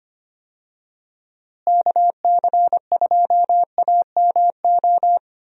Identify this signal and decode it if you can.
Morse code, heard as a keyed tone, about 25 wpm, 700 Hz. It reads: KC2AMO
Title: USA Call Signs - 6 Characters - Repeated 2x - 25wpm